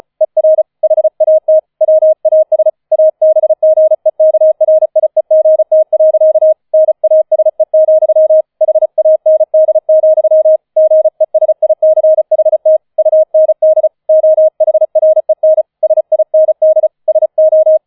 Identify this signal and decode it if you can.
Morse code, heard as a keyed tone, about 34 wpm, 630 Hz. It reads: ER HAT WAS ABGEKRIEGT. NASE, HAND, GESICHT UND OHREN SIND SO
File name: 004 Vierter Streich 35wpm